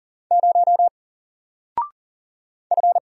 Morse code, heard as a keyed tone, about 40 wpm, 700 Hz. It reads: 0 F
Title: Single Letter-Number 40wpm